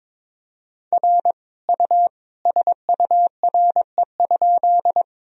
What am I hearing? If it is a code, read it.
Morse code, heard as a keyed tone, about 22 wpm, 700 Hz. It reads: R U SURE?
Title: Sentences from Top 300 Words and Abbreviations 22wpm